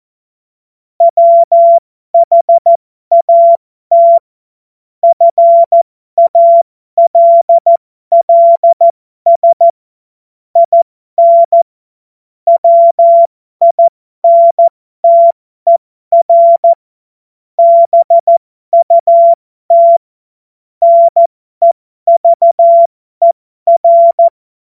Morse code, heard as a keyed tone, about 14 wpm, spaced as for 12 wpm, 680 Hz. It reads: WHAT FALLS IN WINTER BUT NEVER
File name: Morsekurs / Witze